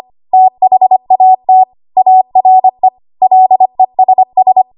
Morse code, heard as a keyed tone, about 25 wpm, 750 Hz. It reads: THAT ARE LESS
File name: Archive / 25 WPM / 25WPM